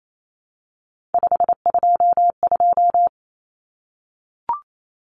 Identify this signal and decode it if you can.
Morse code, heard as a keyed tone, about 28 wpm, 700 Hz. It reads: 522
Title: Common Cut Numbers - 3-Digit 28wpm